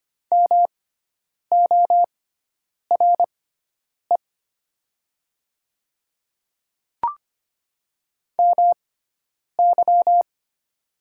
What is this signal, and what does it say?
Morse code, heard as a keyed tone, about 25 wpm, spaced as for 9 wpm, 700 Hz. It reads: MORE MY
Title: 6x Character Spacing - Top 100 Words - 25wpm